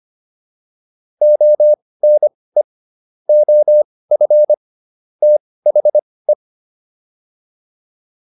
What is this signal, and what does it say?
Morse code, heard as a keyed tone, about 25 wpm, spaced as for 18 wpm, 600 Hz. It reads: ONE OF THE
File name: WD204 18